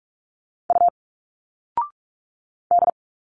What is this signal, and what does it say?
Morse code, heard as a keyed tone, about 46 wpm, 700 Hz. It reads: U D
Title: Single Letters 45wpm